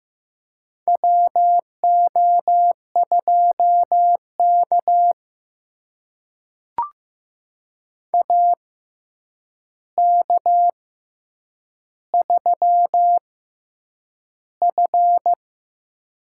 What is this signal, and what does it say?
Morse code, heard as a keyed tone, about 15 wpm, spaced as for 8 wpm, 700 Hz. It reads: WO2K AK3F
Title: USA Call Signs - 4 Characters - 6x Character Spacing - 15wpm